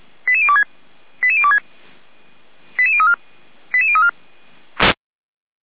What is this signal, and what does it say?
Five-tone selective call (ZVEI), heard as quick runs of five tones, tones of about 70 ms. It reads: 88026 88026 88034 88034